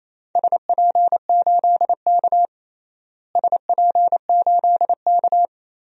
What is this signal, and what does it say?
Morse code, heard as a keyed tone, about 28 wpm, 700 Hz. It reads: SP8K SP8K
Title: DXCC-100 Call Signs - 4 Characters - Repeated 2x - 28wpm